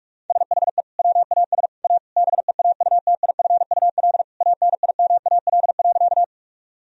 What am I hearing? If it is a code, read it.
Morse code, heard as a keyed tone, about 42 wpm, 700 Hz. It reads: SHE WAS A BEAUTIFUL ANIMAL.
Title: Sentences from Top 1000 Words - Review Part 1 - 45wpm